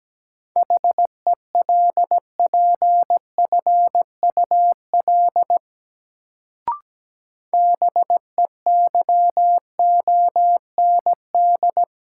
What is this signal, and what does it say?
Morse code, heard as a keyed tone, about 17 wpm, 700 Hz. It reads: HELPFUL BEYOND
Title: Sentences from Top 1000 Words - Review Part 6 - New Word Drill - 17wpm